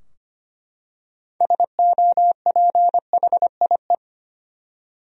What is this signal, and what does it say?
Morse code, heard as a keyed tone, about 25 wpm, 700 Hz. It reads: SOPHIE
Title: USA Common Names - 25wpm